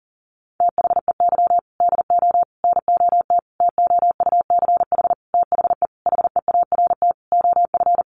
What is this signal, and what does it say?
Morse code, heard as a keyed tone, about 40 wpm, 700 Hz. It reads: THEY DO NOT TOUCH THE HEART OF
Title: Sentences from Top 1200 Words 40wpm